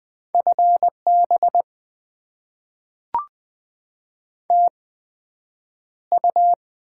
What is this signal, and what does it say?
Morse code, heard as a keyed tone, about 20 wpm, spaced as for 9 wpm, 700 Hz. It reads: FB TU
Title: Top 20 QSO Elements - 8x Character Spacing - 20wpm